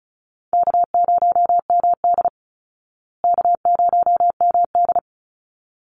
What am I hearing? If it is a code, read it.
Morse code, heard as a keyed tone, about 34 wpm, 700 Hz. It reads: K0MD K0MD
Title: Call Signs Repeated 3x - 35wpm